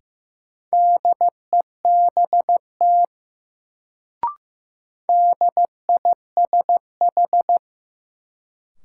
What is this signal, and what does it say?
Morse code, heard as a keyed tone, about 15 wpm, 700 Hz. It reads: DEBT DISH
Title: Sentences from Top 1300 Words - New Word Drill - 15wpm